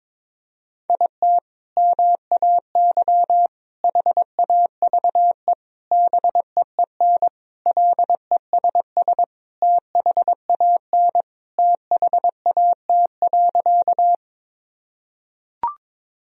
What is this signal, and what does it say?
Morse code, heard as a keyed tone, about 22 wpm, 700 Hz. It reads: IT MAY HAVE BEEN LESS THAN THAT.
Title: Sentences from Top 1000 Words - Review Part 2 - 22wpm